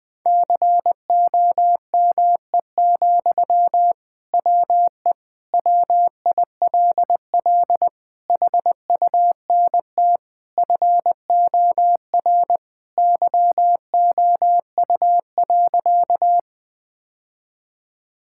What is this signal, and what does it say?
Morse code, heard as a keyed tone, about 21 wpm, 700 Hz. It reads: COME, WE WILL HUNT FOR YOU.
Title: Sentences from Top 1600 Words 20wpm